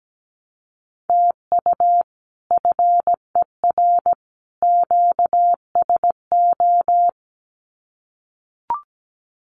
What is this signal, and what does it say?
Morse code, heard as a keyed tone, about 17 wpm, 700 Hz. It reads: TU FER QSO
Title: Common QSO Phrases 17wpm